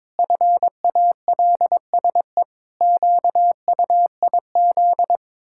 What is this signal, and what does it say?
Morse code, heard as a keyed tone, about 22 wpm, 700 Hz. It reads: FALSE QUIZ
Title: Pangram Challenge v2 - 22wpm